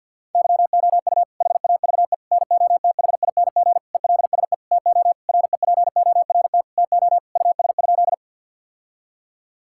Morse code, heard as a keyed tone, about 50 wpm, 700 Hz. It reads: YOU HAVE NOTHING ELSE TO REPORT TO US?
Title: Sentences from Top 1000 Words - Review Part 6 - 50wpm